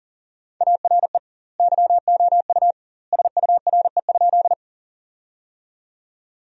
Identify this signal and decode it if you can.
Morse code, heard as a keyed tone, about 40 wpm, 700 Hz. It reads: ARE YOU SURE?
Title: 2x Word-Spacing - Sentences from Top 300 Words - 40wpm